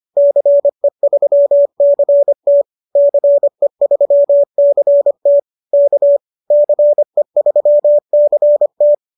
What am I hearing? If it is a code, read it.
Morse code, heard as a keyed tone, about 25 wpm, 590 Hz. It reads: CE3CT CE3CT K CE3CT